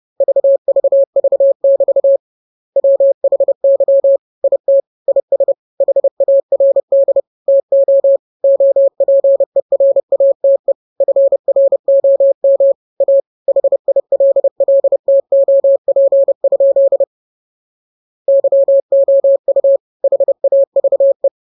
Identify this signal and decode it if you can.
Morse code, heard as a keyed tone, about 30 wpm, 550 Hz. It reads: VVV<BT> WHY IT IS HARD TO OPERATE FROM A HILLTOP? YOU HAVE
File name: jokes / joke 5 30 0000